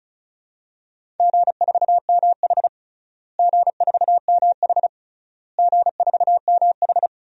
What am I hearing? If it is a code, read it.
Morse code, heard as a keyed tone, about 36 wpm, 700 Hz. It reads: G4MH G4MH G4MH
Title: DXCC-100 Call Signs - 4 Characters - Repeated 3x - 35wpm